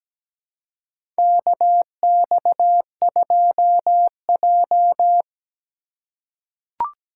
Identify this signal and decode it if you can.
Morse code, heard as a keyed tone, about 17 wpm, 700 Hz. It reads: KX2J
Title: USA Call Signs - 4 Characters - 3x Character Spacing - 17wpm